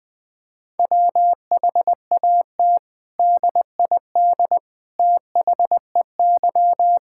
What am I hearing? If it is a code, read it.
Morse code, heard as a keyed tone, about 21 wpm, 700 Hz. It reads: WHAT DID THEY
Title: Sentences from Top 1000 Words - Review Part 5 - 20wpm